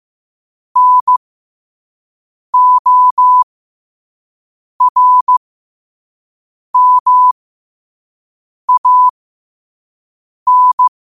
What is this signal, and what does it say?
Morse code, heard as a keyed tone, about 15 wpm, spaced as for 5 wpm, 1000 Hz. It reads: NORMAN
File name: qotdfiles / 5wpm